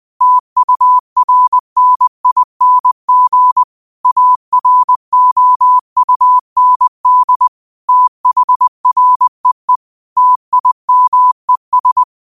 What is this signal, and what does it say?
Morse code, heard as a keyed tone, about 20 wpm, 1000 Hz. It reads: TURNING AROUND THREE TIMES
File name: qotdfiles / 20wpm